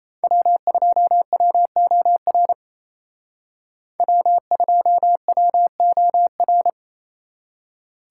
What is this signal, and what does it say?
Morse code, heard as a keyed tone, about 33 wpm, 700 Hz. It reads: W2WOR W2WOR
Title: Speed Racing - USA 1x3 Call Signs 28wpm